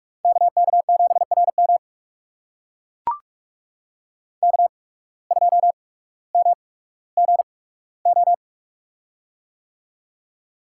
Morse code, heard as a keyed tone, about 44 wpm, spaced as for 19 wpm, 700 Hz. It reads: KK7RM K2MGO
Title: USA Call Signs - 5 Characters - 8x Character Spacing - 45wpm